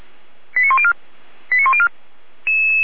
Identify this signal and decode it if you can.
Five-tone selective call (ZVEI), heard as quick runs of five tones, tones of about 70 ms. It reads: 89104 89104